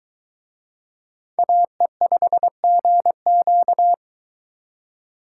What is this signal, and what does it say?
Morse code, heard as a keyed tone, about 22 wpm, 700 Hz. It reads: AE5GQ